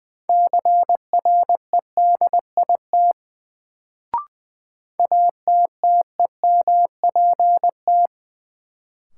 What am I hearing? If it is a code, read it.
Morse code, heard as a keyed tone, about 20 wpm, 700 Hz. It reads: CREDIT ATTEMPT
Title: Sentences from Top 1000 Words - New Word Drill - 20wpm